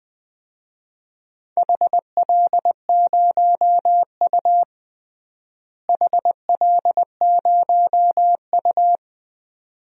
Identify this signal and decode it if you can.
Morse code, heard as a keyed tone, about 20 wpm, 700 Hz. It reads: HL0U HL0U
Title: DXCC-100 Call Signs - 4 Characters - Repeated 2x - 20wpm